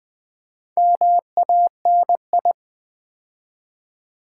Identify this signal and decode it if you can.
Morse code, heard as a keyed tone, about 20 wpm, 700 Hz. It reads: MANI
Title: Build A Word - 4 Letters At A Time - 10-Letter Words - 20wpm